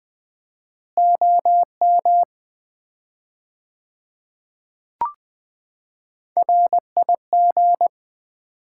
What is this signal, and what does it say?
Morse code, heard as a keyed tone, about 21 wpm, 700 Hz. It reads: OM RIG
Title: Common QSO Phrases - 4x Word-Spacing - 20wpm